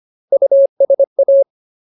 Morse code, heard as a keyed tone, about 25 wpm, 550 Hz. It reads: USA
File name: Ultimate Word List / 25WPM Ultimate